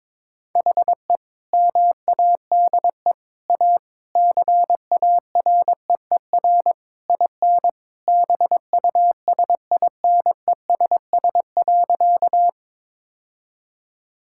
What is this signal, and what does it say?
Morse code, heard as a keyed tone, about 22 wpm, 700 Hz. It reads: HE MADE A CAREER IN BUSINESS.
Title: Sentences from Top 1000 Words - Review Part 6 - 22wpm